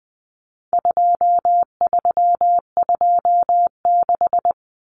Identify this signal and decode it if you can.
Morse code, heard as a keyed tone, about 20 wpm, 700 Hz. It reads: 2326
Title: Common Cut Numbers - 4-Digit 20wpm